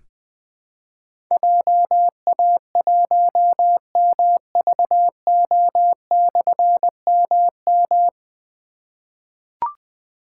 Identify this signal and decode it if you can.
Morse code, heard as a keyed tone, about 20 wpm, 700 Hz. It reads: JA1MVO/MM
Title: Difficult Call Signs 20wpm